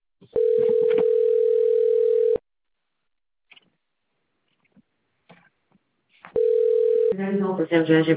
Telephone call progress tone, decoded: ringback tone